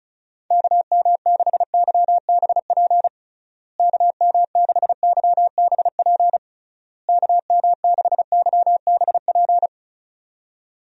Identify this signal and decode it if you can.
Morse code, heard as a keyed tone, about 36 wpm, 700 Hz. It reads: KM6YBP KM6YBP KM6YBP
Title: USA Call Signs - 6 Characters - Repeated 3x - 35wpm